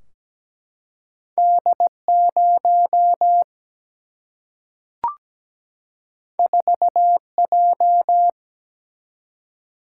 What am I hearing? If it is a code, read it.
Morse code, heard as a keyed tone, about 17 wpm, 700 Hz. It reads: D0 4J
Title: Single-Letter Single-Number 17wpm